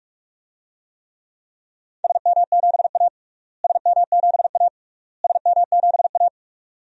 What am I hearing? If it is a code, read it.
Morse code, heard as a keyed tone, about 44 wpm, 700 Hz. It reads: SM7A SM7A SM7A
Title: DXCC-100 Call Signs - 4 Characters - Repeated 3x - 45wpm